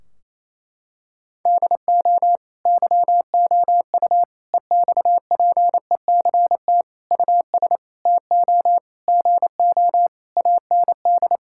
Morse code, heard as a keyed tone, about 27 wpm, 700 Hz. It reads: DO YOU EXPECT US TO GO AND
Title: Sentences from Top 1600 Words 28wpm